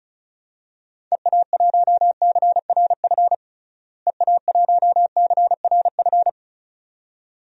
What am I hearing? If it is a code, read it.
Morse code, heard as a keyed tone, about 34 wpm, 700 Hz. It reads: EA1CRF EA1CRF
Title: DXCC-100 Call Signs - 6 Characters - Repeated 2x - 35wpm